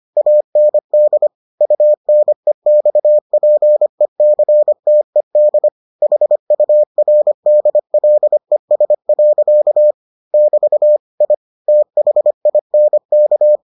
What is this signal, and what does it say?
Morse code, heard as a keyed tone, about 25 wpm, 600 Hz. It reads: AND UNEXPECTED HURDLES. <BT> I THINK